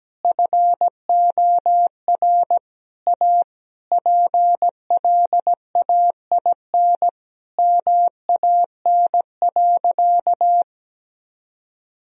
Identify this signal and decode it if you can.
Morse code, heard as a keyed tone, about 17 wpm, 700 Hz. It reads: FOR A PLAIN MAN.
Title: Sentences from Top 1000 Words - Review Part 2 - 17wpm